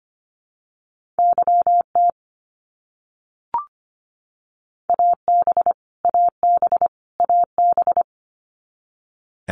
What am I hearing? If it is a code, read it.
Morse code, heard as a keyed tone, about 25 wpm, 700 Hz. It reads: YT AB AB AB
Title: Canadian Provinces and Territories - Repeated 3x - 25wpm